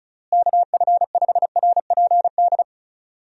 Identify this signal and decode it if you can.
Morse code, heard as a keyed tone, about 34 wpm, 700 Hz. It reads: KF5RPD
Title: USA Call Signs - 6 Characters - Repeated 3x - 35wpm